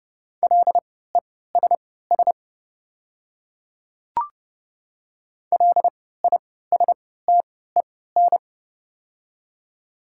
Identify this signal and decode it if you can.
Morse code, heard as a keyed tone, about 30 wpm, spaced as for 17 wpm, 700 Hz. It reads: LESS LISTEN
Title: 3x Character Spacing - Top 500 Words - 30wpm